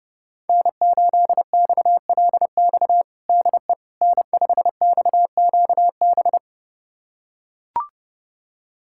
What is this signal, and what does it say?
Morse code, heard as a keyed tone, about 30 wpm, 700 Hz. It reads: N8XLX DE N5XQB
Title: USA Callsign DE Callsign - 5 Characters - 30wpm